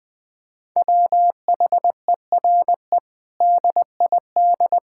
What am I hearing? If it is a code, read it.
Morse code, heard as a keyed tone, about 21 wpm, 700 Hz. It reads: WHERE DID
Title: Sentences from Top 1600 Words 20wpm